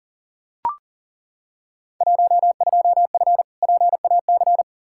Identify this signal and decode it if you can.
Morse code, heard as a keyed tone, about 40 wpm, 700 Hz. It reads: 12F PAC
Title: Field Day Exchanges Encore - 40wpm